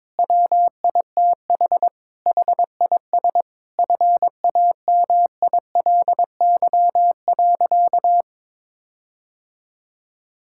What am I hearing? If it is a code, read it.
Morse code, heard as a keyed tone, about 22 wpm, 700 Hz. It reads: WITH HIS FAMILY.
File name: Sentences with 7-Word Phrases - 22wpm